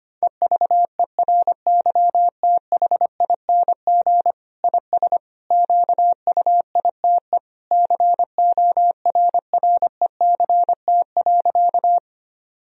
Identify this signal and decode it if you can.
Morse code, heard as a keyed tone, about 25 wpm, 700 Hz. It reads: EVERYTHING IS QUITE CORRECT.
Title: Sentences from Top 1000 Words - Review Part 2 - 25wpm